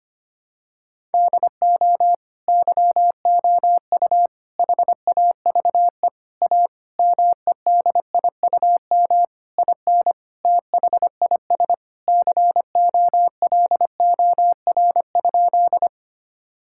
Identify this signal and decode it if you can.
Morse code, heard as a keyed tone, about 25 wpm, 700 Hz. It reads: DO YOU HAVE A MEDIUM IN THIS COLOR?
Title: Sentences from Top 1000 Words - Review Part 6 - 25wpm